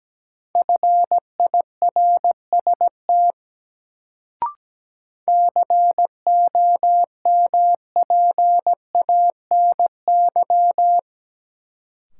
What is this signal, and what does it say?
Morse code, heard as a keyed tone, about 17 wpm, 700 Hz. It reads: FIRST COMPANY
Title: Sets of 2 Words - New Word Drill - 17wpm